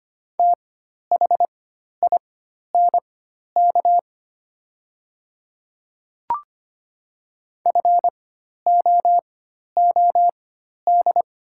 Morse code, heard as a keyed tone, about 25 wpm, spaced as for 12 wpm, 700 Hz. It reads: THINK FOOD
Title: 4x Character Spacing - Top 300 Words - 25wpm